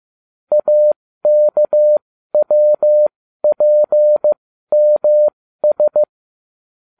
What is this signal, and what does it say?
Morse code, heard as a keyed tone, about 15 wpm, spaced as for 12 wpm, 600 Hz. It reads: AKWPMS